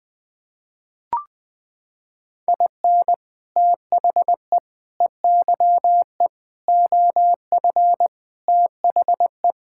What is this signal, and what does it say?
Morse code, heard as a keyed tone, about 20 wpm, 700 Hz. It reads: IN THE EYE OF THE
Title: Sets of 6 Words - 20wpm